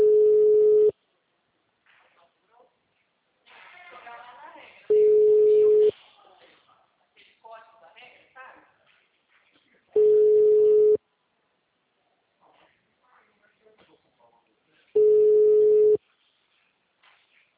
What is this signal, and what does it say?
Ringback tone (Europe, 425 Hz), with speech.
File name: out- -2015- - - .5095